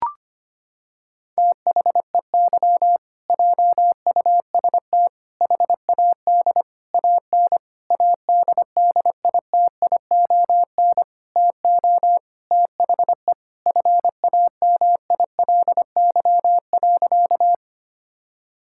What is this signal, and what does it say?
Morse code, heard as a keyed tone, about 25 wpm, 700 Hz. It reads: THEY JUST HAD AN ADDITION TO THE FAMILY.
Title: Sentences from Top 1000 Words - Review Part 6 - 25wpm